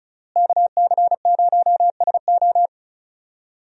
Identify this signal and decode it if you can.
Morse code, heard as a keyed tone, about 34 wpm, 700 Hz. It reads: KC0SO
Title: USA Call Signs - 5 Characters - Repeated 2x - 35wpm